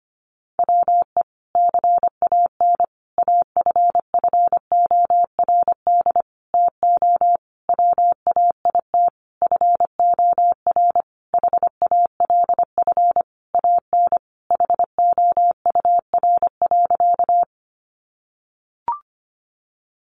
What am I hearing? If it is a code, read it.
Morse code, heard as a keyed tone, about 25 wpm, 700 Hz. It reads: WE CAN AFFORD TO WAIT FOR HALF AN HOUR.
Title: Sentences from Top 1100 Words 25wpm